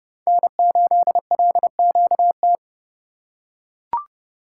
Morse code, heard as a keyed tone, about 31 wpm, 700 Hz. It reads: N8LQT
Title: USA Call Signs - 5 Characters - 3x Character Spacing - 30wpm